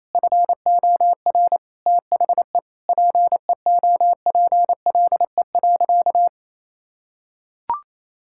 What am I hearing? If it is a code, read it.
Morse code, heard as a keyed tone, about 28 wpm, 700 Hz. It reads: FOR THE PEOPLE.
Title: Sentences from Top 1000 Words - Review Part 5 - 28wpm